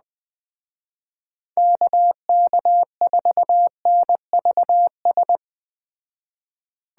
Morse code, heard as a keyed tone, about 21 wpm, 700 Hz. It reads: KK4NVS